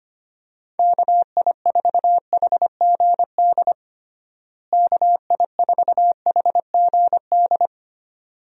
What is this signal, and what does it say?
Morse code, heard as a keyed tone, about 25 wpm, 700 Hz. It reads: KI4HGD KI4HGD
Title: USA Call Signs - 6 Characters - Repeated 2x - 25wpm